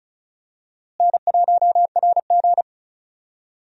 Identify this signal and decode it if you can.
Morse code, heard as a keyed tone, about 36 wpm, 700 Hz. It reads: N1RG
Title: USA Call Signs - 4 Characters - Repeated 2x - 35wpm